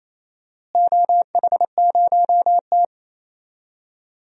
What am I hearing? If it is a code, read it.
Morse code, heard as a keyed tone, about 28 wpm, 700 Hz. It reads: OH0T
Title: DXCC-100 Call Signs - 4 Characters - Repeated 2x - 28wpm